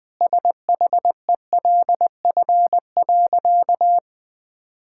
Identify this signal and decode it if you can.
Morse code, heard as a keyed tone, about 20 wpm, 700 Hz. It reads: SHELF.
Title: Sentences with 2-Word Phrases - 20wpm